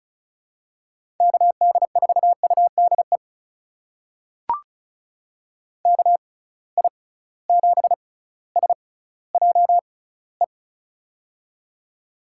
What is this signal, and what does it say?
Morse code, heard as a keyed tone, about 36 wpm, spaced as for 18 wpm, 700 Hz. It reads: KD4UDE KI7SJE
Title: USA Call Signs - 6 Characters - 6x Character Spacing - 35wpm